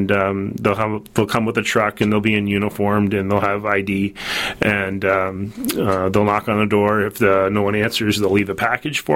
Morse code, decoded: none